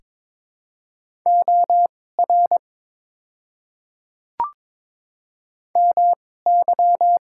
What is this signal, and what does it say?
Morse code, heard as a keyed tone, about 21 wpm, spaced as for 16 wpm, 700 Hz. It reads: OR MY
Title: 2x Character Spacing - Top 100 Words - 22wpm